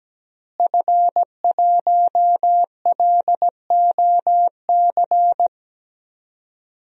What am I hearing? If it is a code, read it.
Morse code, heard as a keyed tone, about 17 wpm, 700 Hz. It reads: F1LOC
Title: DXCC-100 Call Signs - 5 Characters - Repeated 3x - 17wpm